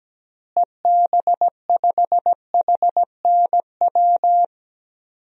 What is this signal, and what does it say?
Morse code, heard as a keyed tone, about 17 wpm, 700 Hz. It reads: EB5HNW
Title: DXCC-100 Call Signs - 6 Characters - Repeated 2x - 17wpm